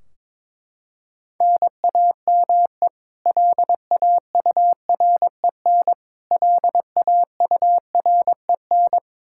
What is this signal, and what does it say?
Morse code, heard as a keyed tone, about 22 wpm, 700 Hz. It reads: NAME LAUREN LAUREN
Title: Practice QSO - Exchange 1 - Beginning - 22wpm